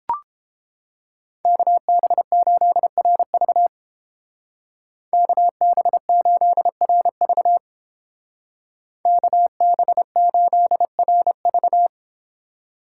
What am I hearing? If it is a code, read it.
Morse code, heard as a keyed tone, about 30 wpm, 700 Hz. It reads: KB8RV KB8RV KB8RV